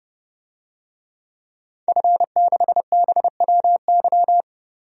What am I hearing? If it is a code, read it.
Morse code, heard as a keyed tone, about 31 wpm, 700 Hz. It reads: F6BWY
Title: DXCC-100 Call Signs - 5 Characters - 30wpm